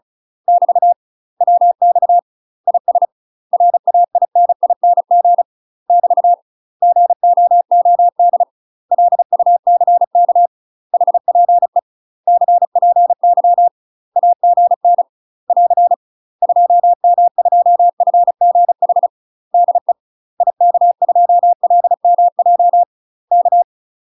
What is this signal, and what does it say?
Morse code, heard as a keyed tone, about 36 wpm, 700 Hz. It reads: <BT> WX IS RAINING <BT> GOOD LUCK HPE CPY AGN <AR> 2M2FGH DE IK2LMJ K
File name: UK MASTER QSOs TRAINING/35WPM